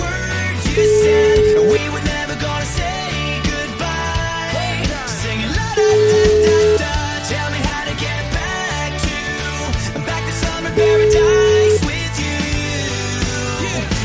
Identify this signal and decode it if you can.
Ringback tone (Europe, 425 Hz), with noise.